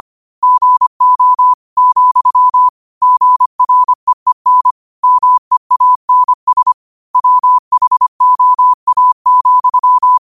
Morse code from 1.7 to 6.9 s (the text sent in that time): , GREEN MEANS